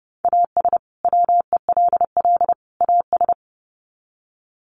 Morse code, heard as a keyed tone, about 30 wpm, 700 Hz. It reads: AS WELL AS
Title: Sets of 3 Words 30wpm